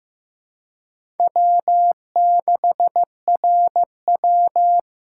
Morse code, heard as a keyed tone, about 15 wpm, 700 Hz. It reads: W6RW
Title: USA Call Signs - 4 Characters - 3x Character Spacing - 15wpm